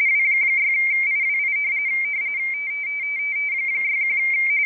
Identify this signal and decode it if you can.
RTTY, about 45.45 baud. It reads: W6IWI 61
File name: rtty / websdr recording 14086.0kHz